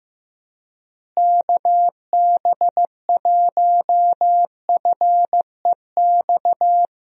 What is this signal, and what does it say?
Morse code, heard as a keyed tone, about 16 wpm, 700 Hz. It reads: KB1FEX
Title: USA Call Signs - 6 Characters - Repeated 2x - 15wpm